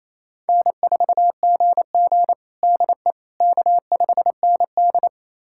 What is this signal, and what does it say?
Morse code, heard as a keyed tone, about 28 wpm, 700 Hz. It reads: N4GG DE K5ND
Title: USA Callsign DE Callsign - 4 Characters - 28wpm